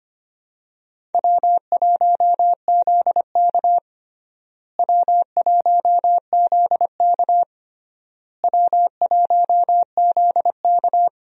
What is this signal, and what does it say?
Morse code, heard as a keyed tone, about 25 wpm, 700 Hz. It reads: W1ZK W1ZK W1ZK
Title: USA Call Signs - 4 Characters - Repeated 3x - 25wpm